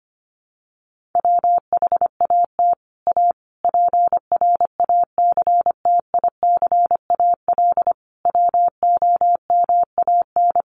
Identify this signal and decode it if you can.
Morse code, heard as a keyed tone, about 25 wpm, 700 Hz. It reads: WHAT A PRACTICAL WOMAN
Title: Sentences from Top 1400 Words 25wpm